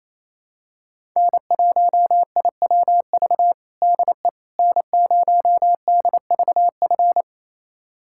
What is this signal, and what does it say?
Morse code, heard as a keyed tone, about 27 wpm, 700 Hz. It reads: N1IWV DE N0DVF
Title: USA Callsign DE Callsign - 5 Characters - 28wpm